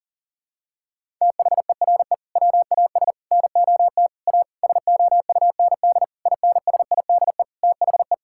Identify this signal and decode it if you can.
Morse code, heard as a keyed tone, about 37 wpm, 700 Hz. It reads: THERE WAS NOT A SOUND INSIDE THE
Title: Sentences from Top 1000 Words - Review Part 6 - 40wpm